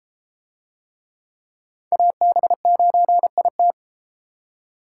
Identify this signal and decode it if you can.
Morse code, heard as a keyed tone, about 31 wpm, 700 Hz. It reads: AB9IT